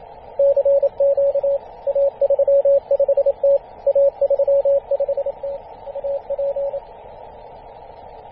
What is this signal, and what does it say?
Morse code, heard as a keyed tone, about 27 wpm, 560 Hz, with radio noise. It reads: CQ A35T A35T AP
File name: posts / 2016-feb-27-0448z-a35t-24-898mhz-cw